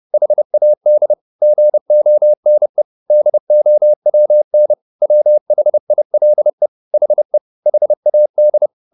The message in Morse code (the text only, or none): HAD GONE DOWN WHILE HE HAD